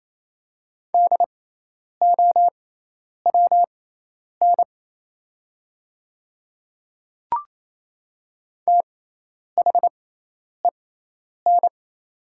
Morse code, heard as a keyed tone, about 27 wpm, spaced as for 10 wpm, 700 Hz. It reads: DOWN THEN
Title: 6x Character Spacing - Top 300 Words - 28wpm